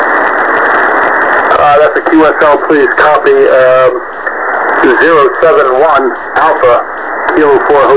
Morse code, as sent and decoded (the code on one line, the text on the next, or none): none